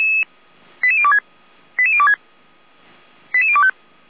Five-tone selective call (ZVEI), heard as quick runs of five tones, tones of about 70 ms. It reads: 88026 88026 88025
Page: /455/SDR/ZVEI/